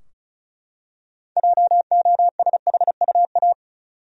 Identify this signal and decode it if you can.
Morse code, heard as a keyed tone, about 34 wpm, 700 Hz. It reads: JOSHUA
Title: USA Common Names - 35wpm